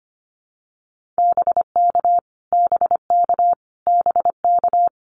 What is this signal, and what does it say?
Morse code, heard as a keyed tone, about 25 wpm, 700 Hz. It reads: BK BK BK
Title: Abbreviations Repeated 3x - 25wpm